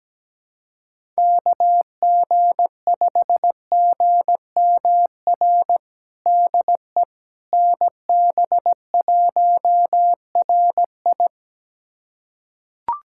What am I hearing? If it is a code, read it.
Morse code, heard as a keyed tone, about 17 wpm, 700 Hz. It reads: KG5GMR DE NB1RI
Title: USA Callsign DE Callsign 17wpm